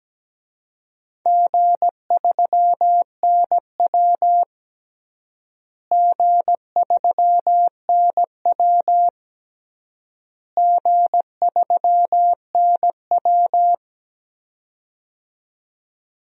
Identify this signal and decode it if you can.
Morse code, heard as a keyed tone, about 17 wpm, 700 Hz. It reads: G3NW G3NW G3NW
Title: DXCC-100 Call Signs - 4 Characters - Repeated 3x - 17wpm